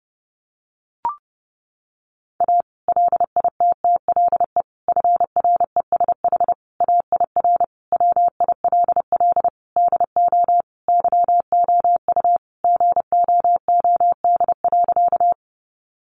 Morse code, heard as a keyed tone, about 30 wpm, 700 Hz. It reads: A LITTLE FRESH AIR WILL DO YOU GOOD.
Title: Sentences from Top 1200 Words 30wpm